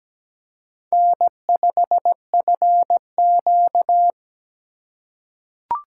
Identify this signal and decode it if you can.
Morse code, heard as a keyed tone, about 17 wpm, 700 Hz. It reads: N5FQ
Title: USA Call Signs - 4 Characters - 4x Character Spacing - 17wpm